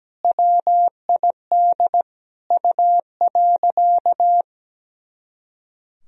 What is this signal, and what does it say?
Morse code, heard as a keyed tone, about 17 wpm, 700 Hz. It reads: WID U.
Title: Sentences from Top 300 Words and Abbreviations 17wpm